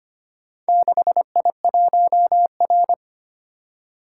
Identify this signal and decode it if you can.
Morse code, heard as a keyed tone, about 25 wpm, 700 Hz. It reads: 6I1R
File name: practice / 25wpm / Four Letter-Number Combinations - 25wpm